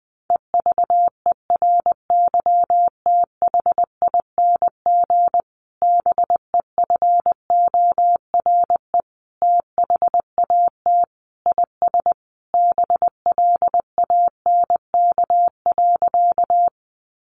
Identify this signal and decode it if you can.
Morse code, heard as a keyed tone, about 20 wpm, 700 Hz. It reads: EVERYTHING BEFORE THAT IS BLANK.
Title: Sentences from Top 1500 Words 20wpm